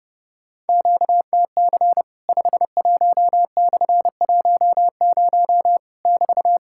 Morse code, heard as a keyed tone, about 30 wpm, 700 Hz. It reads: QTC 51/10 <BT>
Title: Worked All Europe DX Contest - QTC Exchange - 30wpm